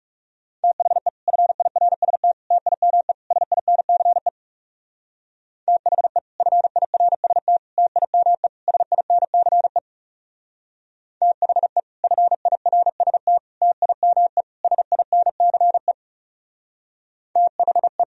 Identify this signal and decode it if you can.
Morse code, heard as a keyed tone, about 37 wpm, 700 Hz. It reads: THE FIRST TIME SINCE THE FIRST TIME SINCE THE FIRST TIME SINCE THE
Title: Speed-Racing - Sets of 4 Words 30wpm